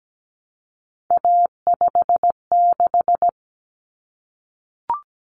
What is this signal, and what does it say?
Morse code, heard as a keyed tone, about 17 wpm, 700 Hz. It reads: A56
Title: Common Cut Numbers - 3-Digit 17wpm